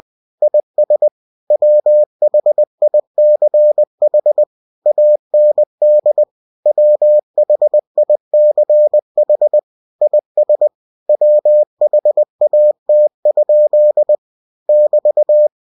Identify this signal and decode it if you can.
Morse code, heard as a keyed tone, about 20 wpm, 600 Hz. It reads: IS WHICH AND WHICH IS WHAT? <BT>
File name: cw-books / winnie-the-pooh / winnie-20wpm-page-0068